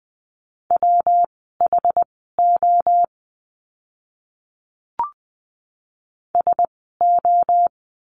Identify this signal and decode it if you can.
Morse code, heard as a keyed tone, about 20 wpm, spaced as for 15 wpm, 700 Hz. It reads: WHO SO
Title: 146 Lesson 24f - Letter y - Words 20wpm